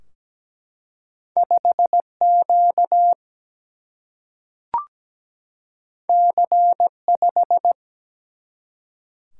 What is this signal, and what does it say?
Morse code, heard as a keyed tone, about 17 wpm, 700 Hz. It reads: 5Q C5
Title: Single-Letter Single-Number 17wpm